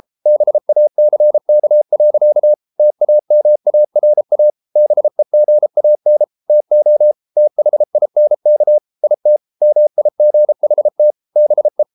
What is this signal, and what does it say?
Morse code, heard as a keyed tone, about 33 wpm, 600 Hz. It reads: BACK. TAMARA BEGAN TO THINK IT MIGHT BE